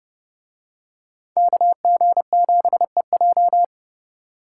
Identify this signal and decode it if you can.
Morse code, heard as a keyed tone, about 28 wpm, 700 Hz. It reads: KG7EJ